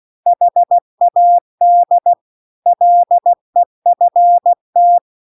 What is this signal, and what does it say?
Morse code, heard as a keyed tone, about 16 wpm, 700 Hz. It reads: HAD LEFT